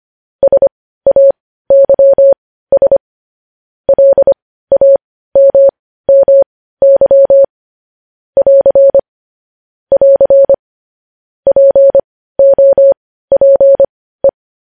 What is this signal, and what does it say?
Morse code, heard as a keyed tone, about 25 wpm, spaced as for 15 wpm, 550 Hz. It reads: SAYS LAMMY <AR> <AR> POPE
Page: news-headlines-in-morse-code-at-15-wpm